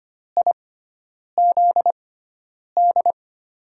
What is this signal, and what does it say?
Morse code, heard as a keyed tone, about 25 wpm, spaced as for 9 wpm, 700 Hz. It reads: IZD
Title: USA Call Signs - 6 Characters - 6x Character Spacing - 25wpm